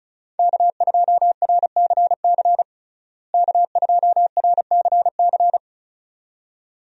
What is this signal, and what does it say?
Morse code, heard as a keyed tone, about 36 wpm, 700 Hz. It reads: K2RCC K2RCC
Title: USA Call Signs - 5 Characters - Repeated 2x - 35wpm